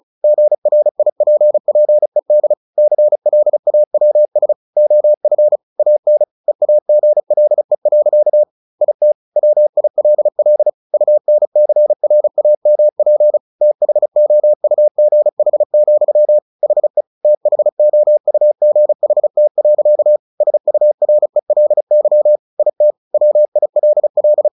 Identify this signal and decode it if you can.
Morse code, heard as a keyed tone, about 34 wpm, 600 Hz. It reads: GRIPPED CLAWS OF AN EAGLE. IT WILL UNCRAMP THOUGH, HE THOUGHT. SURELY IT WILL
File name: the old man and the sea / the old man and the sea 1 35wpm